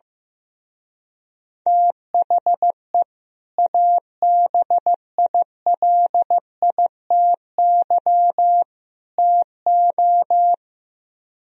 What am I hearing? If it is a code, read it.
Morse code, heard as a keyed tone, about 15 wpm, 700 Hz. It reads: THE ABILITY TO
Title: Speed-Racing - Sets of 3 Words 15wpm